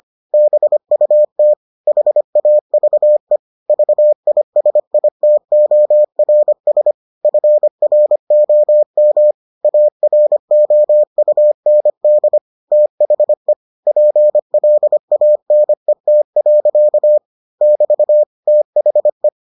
Morse code, heard as a keyed tone, about 25 wpm, 600 Hz. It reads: BUT HAVE VISITORS FROM AROUND THE PLANET. <BT> THE